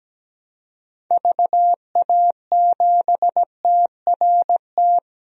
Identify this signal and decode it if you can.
Morse code, heard as a keyed tone, about 17 wpm, 700 Hz. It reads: VA7TRT
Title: DXCC-100 Call Signs - 6 Characters - 17wpm